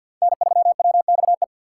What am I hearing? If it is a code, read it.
Morse code, heard as a keyed tone, about 53 wpm, 700 Hz. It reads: N3WXE
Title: USA Call Signs - 2x Character Spacing - 50wpm